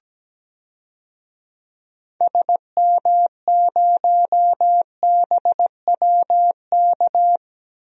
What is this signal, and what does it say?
Morse code, heard as a keyed tone, about 17 wpm, 700 Hz. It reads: SM0BWK
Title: DXCC-100 Call Signs - 6 Characters - Repeated 3x - 17wpm